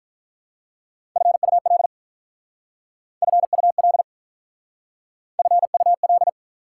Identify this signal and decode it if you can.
Morse code, heard as a keyed tone, about 46 wpm, 700 Hz. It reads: FUL FUL FUL